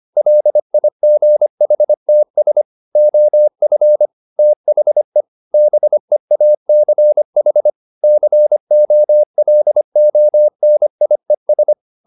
Morse code, heard as a keyed tone, about 25 wpm, 600 Hz. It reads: LIGHTS OF THE BEACH COLONIES